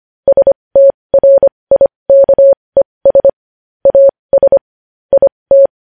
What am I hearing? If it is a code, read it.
Morse code, heard as a keyed tone, about 25 wpm, spaced as for 20 wpm, 550 Hz. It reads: STRIKES AS IT
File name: podcast / News headlines-1x-25x20